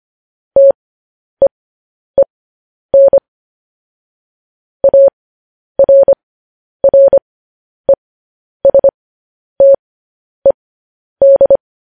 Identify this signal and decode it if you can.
Morse code, heard as a keyed tone, about 25 wpm, spaced as for 10 wpm, 550 Hz. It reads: TEEN ARRESTED